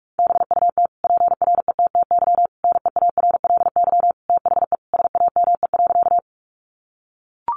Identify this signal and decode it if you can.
Morse code, heard as a keyed tone, about 44 wpm, 700 Hz. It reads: BUT PRETTY NEARLY THE SAME.
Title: Sentences from Top 1300 Words 45wpm